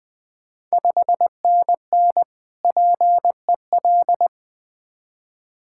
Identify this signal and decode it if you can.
Morse code, heard as a keyed tone, about 20 wpm, 700 Hz. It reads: 5NN PEL
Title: Ontario QSO Party Exchange 20wpm